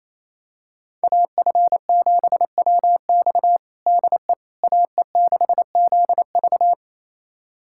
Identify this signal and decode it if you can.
Morse code, heard as a keyed tone, about 28 wpm, 700 Hz. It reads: AF7WX DE AE6ZV
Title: USA Callsign DE Callsign - 5 Characters - 28wpm